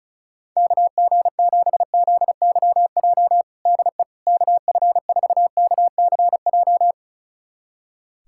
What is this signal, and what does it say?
Morse code, heard as a keyed tone, about 36 wpm, 700 Hz. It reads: KG7ZYJ DE KF4KCJ
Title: USA Callsign DE Callsign - 6 Characters - 35wpm